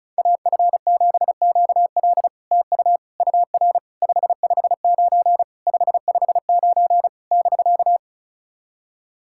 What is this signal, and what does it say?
Morse code, heard as a keyed tone, about 37 wpm, 700 Hz. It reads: AF7QL TU UR 559 559 <BK>
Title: SOTA QSO Practice 35wpm